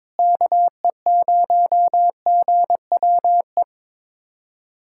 Morse code, heard as a keyed tone, about 22 wpm, 700 Hz. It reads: KE0GWE
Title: USA Callsign DE Callsign 22wpm